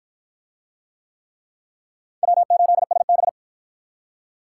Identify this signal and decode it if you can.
Morse code, heard as a keyed tone, about 53 wpm, 700 Hz. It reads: W8SB